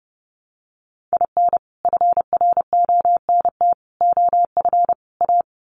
Morse code, heard as a keyed tone, about 30 wpm, 700 Hz. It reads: IN FRONT OF A
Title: Sets of 4 Words 30wpm